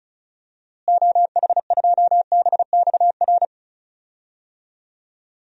Morse code, heard as a keyed tone, about 34 wpm, 700 Hz. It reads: OH2BXR
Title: DXCC-100 Call Signs - 6 Characters - Repeated 2x - 35wpm